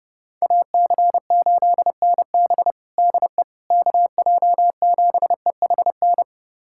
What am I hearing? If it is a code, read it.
Morse code, heard as a keyed tone, about 30 wpm, 700 Hz. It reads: AC8NB DE KJ7EHN
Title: USA Callsign DE Callsign 30wpm